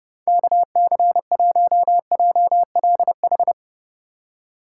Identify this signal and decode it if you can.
Morse code, heard as a keyed tone, about 30 wpm, 700 Hz. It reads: KC1JLH
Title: USA Call Signs - 6 Characters - 3x Character Spacing - 30wpm